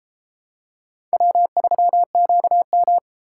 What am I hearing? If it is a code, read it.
Morse code, heard as a keyed tone, about 34 wpm, 700 Hz. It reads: W3QM